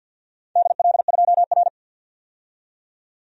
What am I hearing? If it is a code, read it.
Morse code, heard as a keyed tone, about 53 wpm, 700 Hz. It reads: DL2R
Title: DXCC-100 Call Signs - 4 Characters - 50wpm